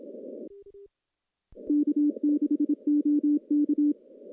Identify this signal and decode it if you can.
Morse code, heard as a keyed tone, about 26 wpm, 290 Hz. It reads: K6OK
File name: CW-SS-k5tr